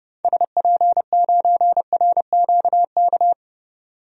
Morse code, heard as a keyed tone, about 30 wpm, 700 Hz. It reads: SP9RQK
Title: DXCC-100 Call Signs - 6 Characters - 30wpm